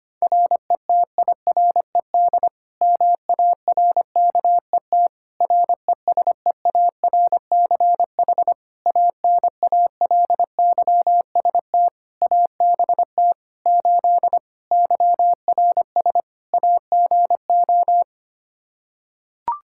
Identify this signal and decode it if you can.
Morse code, heard as a keyed tone, about 25 wpm, 700 Hz. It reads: RETIRED MARKET RESEARCH ANALYST ABT 8 YRS AGO
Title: Practice QSO - Exchange 3 - More Info - 25wpm